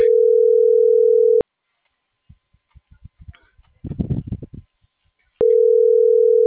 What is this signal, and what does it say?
Ringback tone (North America), 440 + 480 Hz.